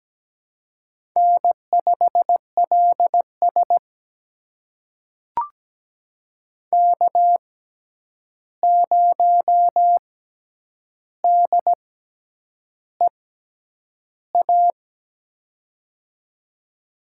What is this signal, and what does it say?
Morse code, heard as a keyed tone, about 17 wpm, spaced as for 8 wpm, 700 Hz. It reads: N5LS K0DEA
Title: USA Call Signs - 6x Character Spacing - 17wpm